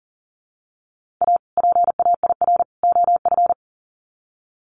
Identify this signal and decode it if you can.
Morse code, heard as a keyed tone, about 40 wpm, 700 Hz. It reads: A PAIR OF
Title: Sets of 3 Words 40wpm